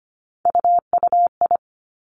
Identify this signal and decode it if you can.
Morse code, heard as a keyed tone, about 25 wpm, 700 Hz. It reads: UUI
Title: Three Challenging Letter-Number Combinations 25wpm